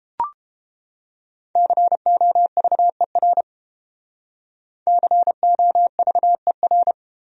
Morse code, heard as a keyed tone, about 31 wpm, 700 Hz. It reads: COVER COVER